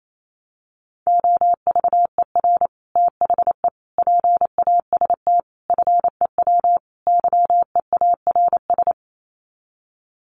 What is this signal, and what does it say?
Morse code, heard as a keyed tone, about 28 wpm, 700 Hz. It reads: OVER THE PAST FEW YEARS
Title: Sets of 5 Words Encore 28wpm